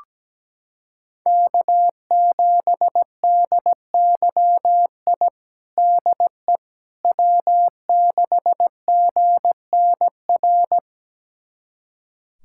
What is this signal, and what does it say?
Morse code, heard as a keyed tone, about 17 wpm, 700 Hz. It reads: K7DYI DE W6GNR
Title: USA Callsign DE Callsign - 5 Characters - 17wpm